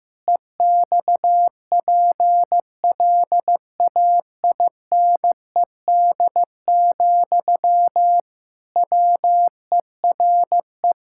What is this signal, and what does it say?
Morse code, heard as a keyed tone, about 14 wpm, 700 Hz. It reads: EXPLAINED, WERE
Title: Pangram Challenge v2 - 15wpm